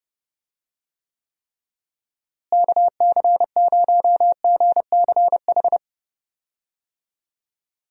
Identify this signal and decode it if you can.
Morse code, heard as a keyed tone, about 30 wpm, 700 Hz. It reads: KC0GCH
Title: USA Call Signs - 6 Characters - Repeated 3x - 30wpm